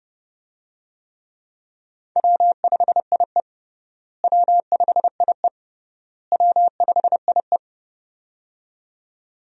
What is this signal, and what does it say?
Morse code, heard as a keyed tone, about 30 wpm, 700 Hz. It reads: W5IE W5IE W5IE
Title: USA Call Signs - 4 Characters - Repeated 3x - 30wpm